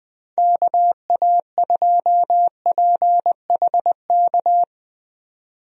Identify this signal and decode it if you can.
Morse code, heard as a keyed tone, about 20 wpm, 700 Hz. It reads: KA2PHK
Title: USA Call Signs - 6 Characters - 20wpm